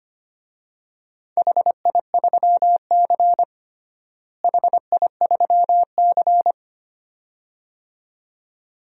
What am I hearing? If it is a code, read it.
Morse code, heard as a keyed tone, about 25 wpm, 700 Hz. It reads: HI3C HI3C
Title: DXCC-100 Call Signs - 4 Characters - Repeated 2x - 25wpm